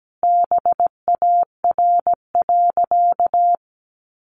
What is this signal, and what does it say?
Morse code, heard as a keyed tone, about 17 wpm, 700 Hz. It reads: BAR.
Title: Easy Phrases 17wpm